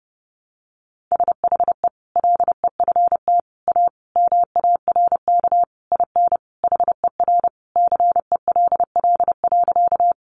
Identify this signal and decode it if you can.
Morse code, heard as a keyed tone, about 30 wpm, 700 Hz. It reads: SHE LEFT A MARK IN HER CELL.
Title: Sentences from Top 1100 Words 30wpm